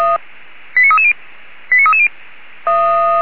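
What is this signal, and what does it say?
Five-tone selective call (ZVEI), heard as quick runs of five tones, tones of about 70 ms. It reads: 89330 89330
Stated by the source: ZVEI five-tone selective call received with an SDR, logged on the Feuerwehr Orlamünde audio server.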